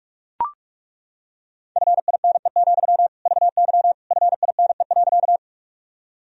Morse code, heard as a keyed tone, about 41 wpm, 690 Hz. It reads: FINE, VY FINE.
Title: Sentences from Top 500 Words and Abbreviations 45wpm